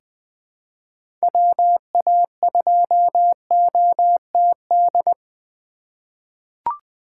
Morse code, heard as a keyed tone, about 20 wpm, 700 Hz. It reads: WA2OTD
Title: USA Call Signs - 6 Characters - 3x Character Spacing - 20wpm